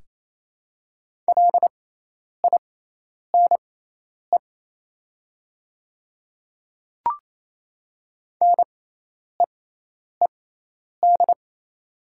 Morse code, heard as a keyed tone, about 27 wpm, spaced as for 10 wpm, 690 Hz. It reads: LINE NEED
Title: 6x Character Spacing - Top 300 Words - 28wpm